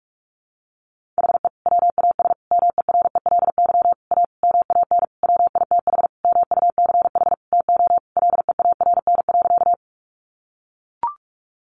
Morse code, heard as a keyed tone, about 44 wpm, 700 Hz. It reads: HE WAS MERELY A MAN WITH MUCH TO LEARN.
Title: Sentences from Top 1400 Words 45wpm